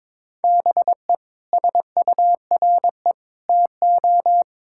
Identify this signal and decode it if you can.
Morse code, heard as a keyed tone, about 22 wpm, 700 Hz. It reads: BE SURE TO
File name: practice / 22wpm / Sets of 6 Words - 22wpm